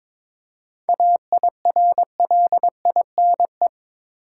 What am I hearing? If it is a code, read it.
Morse code, heard as a keyed tone, about 21 wpm, 700 Hz. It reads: AIRLINE
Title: Seven Letter Words - 22wpm